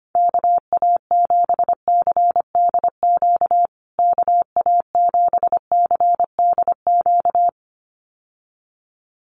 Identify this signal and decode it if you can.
Morse code, heard as a keyed tone, about 25 wpm, 700 Hz. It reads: KA7CDQ KA7CDQ
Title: USA Call Signs Repeated 2x 25wpm